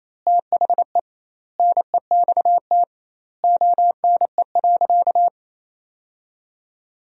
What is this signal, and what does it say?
Morse code, heard as a keyed tone, about 28 wpm, 700 Hz. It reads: THE NEXT ONE.
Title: 2x Word-Spacing - Sentences from Top 400 Words - 28wpm